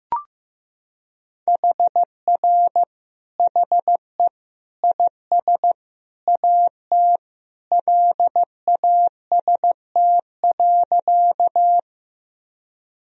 Morse code, heard as a keyed tone, about 15 wpm, 700 Hz. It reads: HR HE IS AT LAST.
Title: Sentences from Top 300 Words and Abbreviations 15wpm